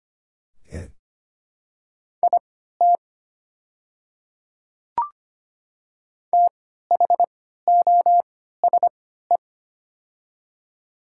Morse code, heard as a keyed tone, about 25 wpm, spaced as for 14 wpm, 700 Hz. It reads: IT THOSE
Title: 3x Character Spacing - Top 300 Words - 25wpm